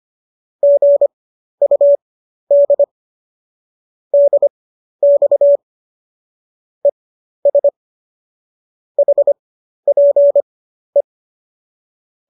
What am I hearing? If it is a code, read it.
Morse code, heard as a keyed tone, about 25 wpm, spaced as for 12 wpm, 570 Hz. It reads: GUD DX ES HPE
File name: qso012 12